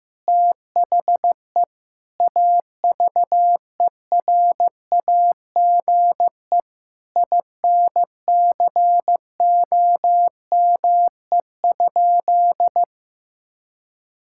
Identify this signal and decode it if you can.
Morse code, heard as a keyed tone, about 16 wpm, 700 Hz. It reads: THE AVERAGE INCOME?
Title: Sentences from Top 1000 Words - Review Part 6 - 15wpm